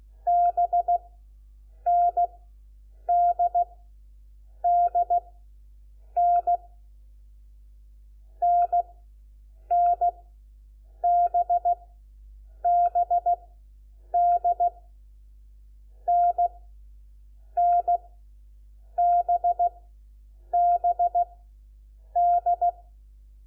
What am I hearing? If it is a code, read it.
Morse code, heard as a keyed tone, about 16 wpm, spaced as for 7 wpm, 700 Hz. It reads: BNDDN NNBBD NNBBD
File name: Lecciones / CLASE Nº4